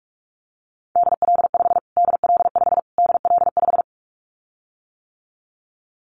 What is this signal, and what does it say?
Morse code, heard as a keyed tone, about 44 wpm, 700 Hz. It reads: DL5 DL5 DL5
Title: Callsign Prefixes for Top 25 DXCC - Repeated 3x - 45wpm